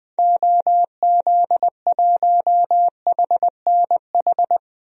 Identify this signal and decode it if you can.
Morse code, heard as a keyed tone, about 21 wpm, 700 Hz. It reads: OZ1HNH
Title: DXCC-100 Call Signs - 6 Characters - Repeated 3x - 20wpm